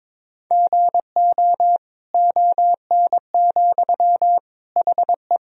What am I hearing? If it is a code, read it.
Morse code, heard as a keyed tone, about 21 wpm, 700 Hz. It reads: GO ON, HE
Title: Sentences from Top 100 Words and Abbreviations 22wpm